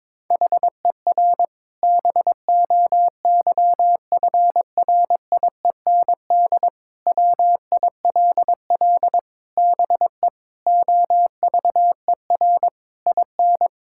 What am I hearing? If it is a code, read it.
Morse code, heard as a keyed tone, about 22 wpm, 700 Hz. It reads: HER BOYFRIEND WILL BE OVER IN
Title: Sentences from Top 1000 Words - Review Part 5 - 22wpm